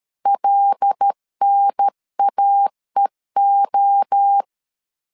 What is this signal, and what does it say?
Morse code, heard as a keyed tone, about 13 wpm, 800 Hz. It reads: LNAEO
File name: LEZIONI / 7 GF9Z / 7 G 12wpm 2055